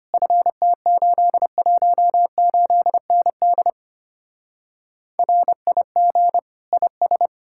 Morse code, heard as a keyed tone, about 32 wpm, 700 Hz. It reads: FT818ND RIG IS